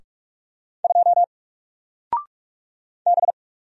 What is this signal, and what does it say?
Morse code, heard as a keyed tone, about 46 wpm, 700 Hz. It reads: 2 B
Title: Single Letter-Number 45wpm